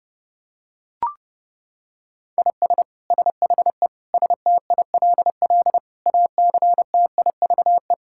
Morse code, heard as a keyed tone, about 32 wpm, 700 Hz. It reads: IS SHE STILL ACTIVE
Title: Sentences from Top 1000 Words - Review Part 6 - 30wpm